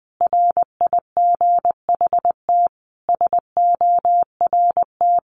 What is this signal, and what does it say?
Morse code, heard as a keyed tone, about 20 wpm, 700 Hz. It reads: RIGHT SORT
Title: Sentences from Top 1300 Words 20wpm